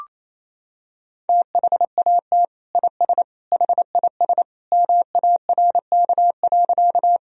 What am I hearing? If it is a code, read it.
Morse code, heard as a keyed tone, about 28 wpm, 700 Hz. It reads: THAT IS HIS MARK.
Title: Sentences from Top 1000 Words - Review Part 1 - 28wpm